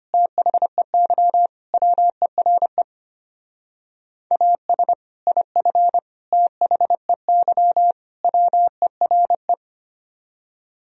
Morse code, heard as a keyed tone, about 28 wpm, 700 Hz. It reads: THEY WERE AS IF THEY WERE
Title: Speed-Racing - Sets of 4 Words 25wpm